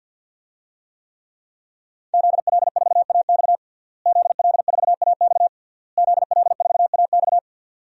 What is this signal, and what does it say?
Morse code, heard as a keyed tone, about 49 wpm, 700 Hz. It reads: ZL4AX ZL4AX ZL4AX
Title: DXCC-100 Call Signs - 5 Characters - Repeated 3x - 50wpm